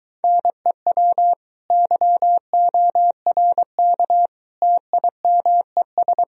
Morse code, heard as a keyed tone, about 23 wpm, 700 Hz. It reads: NEW YORK TIMES